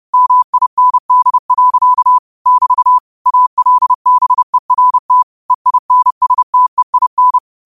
Morse code, heard as a keyed tone, about 30 wpm, 1000 Hz. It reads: MIND. <BT> ALBERT EINSTEIN